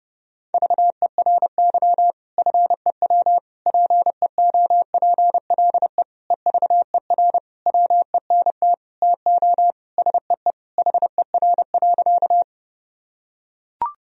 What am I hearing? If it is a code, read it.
Morse code, heard as a keyed tone, about 30 wpm, 700 Hz. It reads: VERY FEW PEOPLE EVER WENT TO SEE HER.
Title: Sentences from Top 1000 Words - Review Part 1 - 30wpm